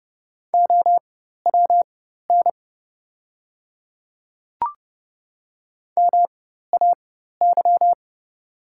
Morse code, heard as a keyed tone, about 31 wpm, spaced as for 14 wpm, 690 Hz. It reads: OWN MAY
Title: 4x Character Spacing - Top 300 Words - 30wpm